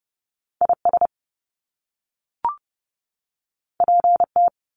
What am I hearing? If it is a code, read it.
Morse code, heard as a keyed tone, about 30 wpm, 700 Hz. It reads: I S P T
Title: Top Two Letter Combinations in Words 30wpm